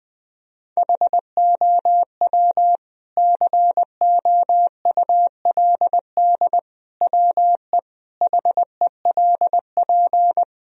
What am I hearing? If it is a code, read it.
Morse code, heard as a keyed tone, about 20 wpm, 700 Hz. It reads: HOW COULD WE HELP
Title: Sentences from Top 1000 Words - Review Part 3 - 20wpm